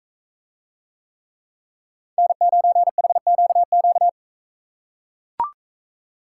Morse code, heard as a keyed tone, about 40 wpm, 700 Hz. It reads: N9HQQ